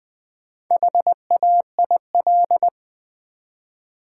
Morse code, heard as a keyed tone, about 20 wpm, 700 Hz. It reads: HAIL
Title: Build A Word - 1 Letter At A Time - 4-Letter Words - 20wpm